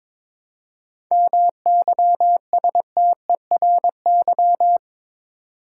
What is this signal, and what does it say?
Morse code, heard as a keyed tone, about 21 wpm, 700 Hz. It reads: MYSTERY
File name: Difficult Words with Hints - 22wpm